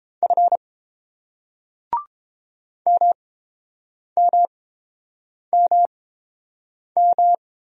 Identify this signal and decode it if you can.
Morse code, heard as a keyed tone, about 30 wpm, spaced as for 7 wpm, 700 Hz. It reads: F MMMM